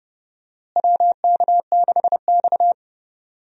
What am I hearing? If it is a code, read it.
Morse code, heard as a keyed tone, about 30 wpm, 700 Hz. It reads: WK6X
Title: USA Call Signs - 4 Characters - Repeated 2x - 30wpm